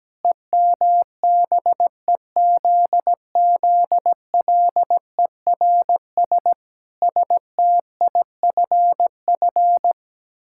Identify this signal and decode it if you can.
Morse code, heard as a keyed tone, about 17 wpm, 700 Hz. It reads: EMBEZZLERS STIFF
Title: Pangram Challenge v2 - 17wpm